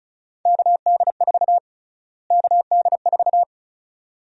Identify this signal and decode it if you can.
Morse code, heard as a keyed tone, about 36 wpm, 700 Hz. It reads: KD4 KD4
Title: USA Call Sign Prefixes - Repeated 2x - 35wpm